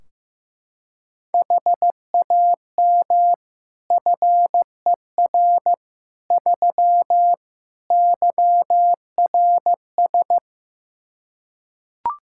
Morse code, heard as a keyed tone, about 15 wpm, 700 Hz. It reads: HAM FER 3 YRS
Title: Practice QSO - Exchange 3 - More Info - 15wpm